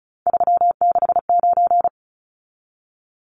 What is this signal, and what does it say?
Morse code, heard as a keyed tone, about 34 wpm, 700 Hz. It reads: 369
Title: Numbers - 3-Digit 35wpm